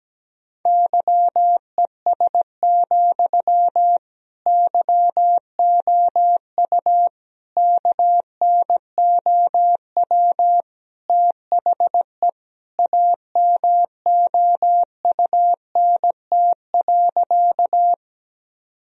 Morse code, heard as a keyed tone, about 17 wpm, 700 Hz. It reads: YES, YOU KNOW THE AMOUNT.
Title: Sentences from Top 1000 Words - Review Part 4 - 17wpm